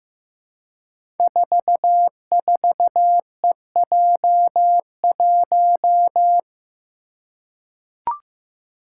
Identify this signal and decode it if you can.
Morse code, heard as a keyed tone, about 15 wpm, 700 Hz. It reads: 44EJ1
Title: Code Groups - 5 Letter:Numbers - 15wpm